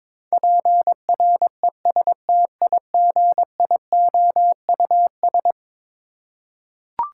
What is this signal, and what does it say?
Morse code, heard as a keyed tone, about 22 wpm, 700 Hz. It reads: PRESTIGIOUS
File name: practice / 22wpm / Eleven Letter Words - 22wpm